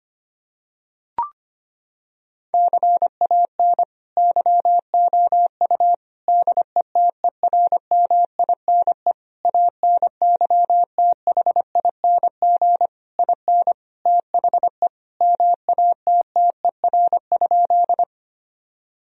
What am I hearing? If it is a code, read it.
Morse code, heard as a keyed tone, about 25 wpm, 700 Hz. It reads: CAN YOU DETERMINE ANYTHING IN THE MATTER?
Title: Sentences from Top 1000 Words - Review Part 4 - 25wpm